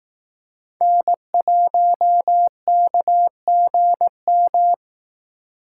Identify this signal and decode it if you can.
Morse code, heard as a keyed tone, about 18 wpm, 700 Hz. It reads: N1KGM
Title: Speed Racing - USA 1x3 Call Signs 15wpm